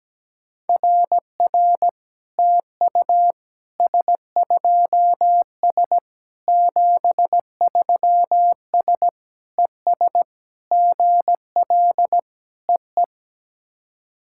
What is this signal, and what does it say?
Morse code, heard as a keyed tone, about 17 wpm, 700 Hz. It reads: RR TU S2S 73S ES GL EE
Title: SOTA QSO Practice 17wpm